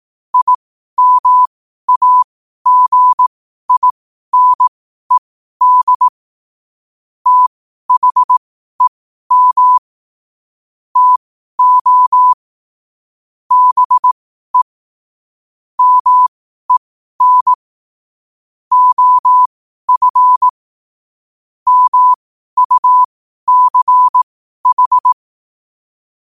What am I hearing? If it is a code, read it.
Morse code, heard as a keyed tone, about 18 wpm, spaced as for 12 wpm, 1000 Hz. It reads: IMAGINED THEM TO BE MEN OF MUCH